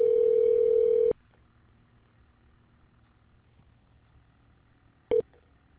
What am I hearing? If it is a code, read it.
Ringback tone (North America), 440 + 480 Hz.